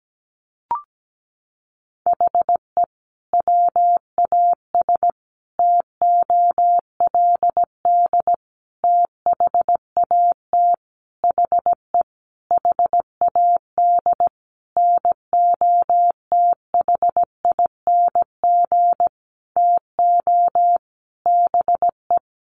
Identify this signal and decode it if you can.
Morse code, heard as a keyed tone, about 17 wpm, 700 Hz. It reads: HE WAS TOLD THAT HE HAD NOTHING TO BE
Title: Sentences from Top 1500 Words 17wpm